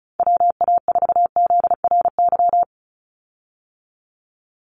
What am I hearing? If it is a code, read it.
Morse code, heard as a keyed tone, about 34 wpm, 700 Hz. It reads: WA4ZRY
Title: USA Call Signs 35wpm